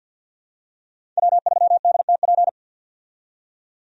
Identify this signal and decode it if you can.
Morse code, heard as a keyed tone, about 45 wpm, 700 Hz. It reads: W3DTP
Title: USA Call Signs - 5 Characters - 50wpm